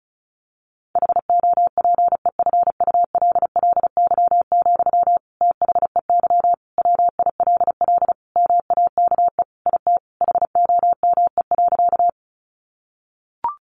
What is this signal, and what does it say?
Morse code, heard as a keyed tone, about 34 wpm, 700 Hz. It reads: HOPEFULLY, THEY WILL MAKE IT HOME.
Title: Sentences from Top 1500 Words 35wpm